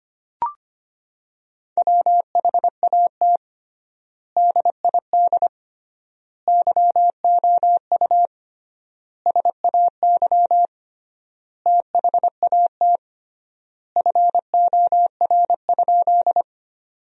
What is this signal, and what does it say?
Morse code, heard as a keyed tone, about 25 wpm, 700 Hz. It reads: WHAT DID YOU SAY THAT FOR?
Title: 3x Word-Spacing - Sentences from Top 200 Words - 25wpm